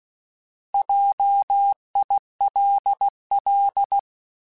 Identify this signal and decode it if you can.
Morse code, heard as a keyed tone, about 16 wpm, 800 Hz. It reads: JILL
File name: Alphabetical / Alphabetical mp3s / Js @15